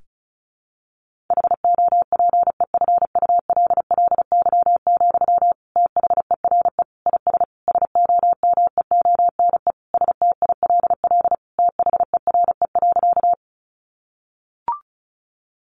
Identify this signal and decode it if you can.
Morse code, heard as a keyed tone, about 34 wpm, 700 Hz. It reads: HOPEFULLY, THERE IS SOMEONE STILL THERE.
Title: Sentences from Top 1500 Words 35wpm